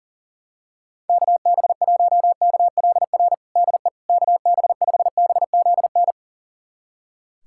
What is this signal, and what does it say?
Morse code, heard as a keyed tone, about 40 wpm, 700 Hz. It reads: KB1KLR DE KB5BZN
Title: USA Callsign DE Callsign - 6 Characters - 40wpm